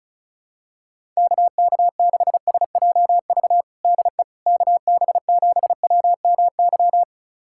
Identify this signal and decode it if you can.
Morse code, heard as a keyed tone, about 36 wpm, 700 Hz. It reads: KK6SJV DE KB7WMY
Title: USA Callsign DE Callsign - 6 Characters - 35wpm